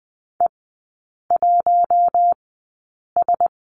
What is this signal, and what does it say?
Morse code, heard as a keyed tone, about 20 wpm, spaced as for 8 wpm, 700 Hz. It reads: E1S
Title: 114 Lesson 19e - Letter w - Three Characters 20wpm